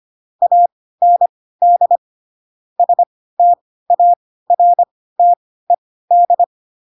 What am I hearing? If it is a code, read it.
Morse code, heard as a keyed tone, about 25 wpm, spaced as for 16 wpm, 700 Hz. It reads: AND STARTED
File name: Pinocchio / Pinocchio ch1 10 25 ik1hge